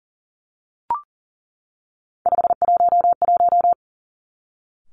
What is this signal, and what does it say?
Morse code, heard as a keyed tone, about 40 wpm, 700 Hz. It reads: 511
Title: Three Challenging Letter-Number Combinations 40wpm